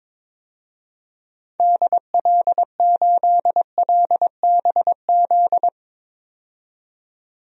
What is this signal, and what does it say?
Morse code, heard as a keyed tone, about 22 wpm, 700 Hz. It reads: DL8LBZ
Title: DXCC-100 Call Signs - 6 Characters - 22wpm